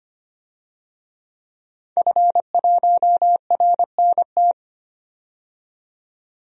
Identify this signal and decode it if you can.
Morse code, heard as a keyed tone, about 25 wpm, 700 Hz. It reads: F1RNT
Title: DXCC-100 Call Signs - 5 Characters - 25wpm